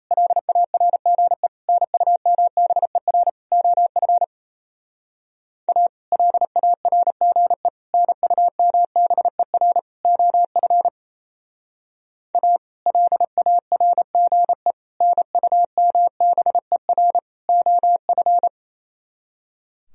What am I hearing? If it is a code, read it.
Morse code, heard as a keyed tone, about 36 wpm, 700 Hz. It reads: LARGE NUMBER OF A LARGE NUMBER OF A LARGE NUMBER OF